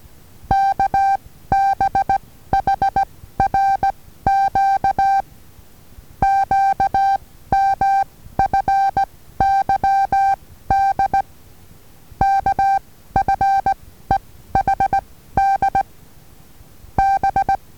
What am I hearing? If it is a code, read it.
Morse code, heard as a keyed tone, about 19 wpm, spaced as for 14 wpm, 780 Hz. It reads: KBHRQ QMFYD K F E H D B
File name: Taviro HA6VA B